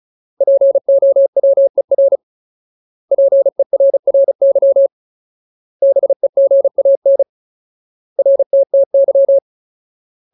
Morse code, heard as a keyed tone, about 34 wpm, 550 Hz. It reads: POWER PERRY BEGAN RTTY
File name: Ultimate Word List / 35WPM Ultimate